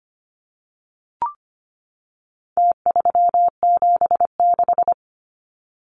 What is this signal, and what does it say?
Morse code, heard as a keyed tone, about 25 wpm, 700 Hz. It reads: T376
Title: Numbers - Contest Style - 4-Digit 25wpm